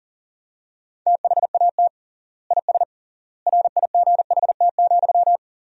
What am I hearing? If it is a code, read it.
Morse code, heard as a keyed tone, about 37 wpm, 700 Hz. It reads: THAT IS RIGHT,
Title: 3x Word-Spacing - Sentences from Top 200 Words - 40wpm